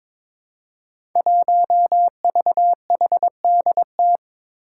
Morse code, heard as a keyed tone, about 22 wpm, 700 Hz. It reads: 1VHDT
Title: Five Challenging Letter-Number Combinations 22wpm